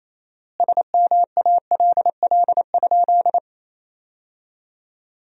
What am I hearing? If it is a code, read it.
Morse code, heard as a keyed tone, about 28 wpm, 700 Hz. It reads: SMALL?
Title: 4x Word-Spacing - Sentences from Top 200 Words - 28wpm